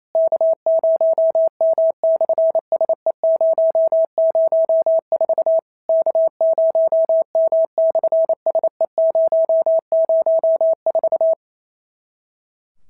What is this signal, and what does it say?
Morse code, heard as a keyed tone, about 28 wpm, 650 Hz. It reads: K0M/SE004 K0M/SE004
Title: SOTA QSO Practice 28wpm